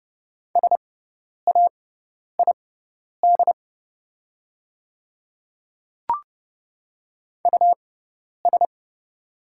Morse code, heard as a keyed tone, about 30 wpm, spaced as for 10 wpm, 700 Hz. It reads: SAID US
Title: 6x Character Spacing - Top 200 Words - 30wpm